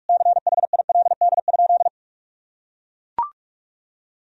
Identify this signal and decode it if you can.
Morse code, heard as a keyed tone, about 43 wpm, 700 Hz. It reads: CHILD?
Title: Sentences from Top 1000 Words - Review Part 1 - 45wpm